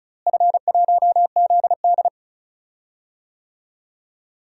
Morse code, heard as a keyed tone, about 36 wpm, 700 Hz. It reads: F1ZD
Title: DXCC-100 Call Signs - 4 Characters - Repeated 2x - 35wpm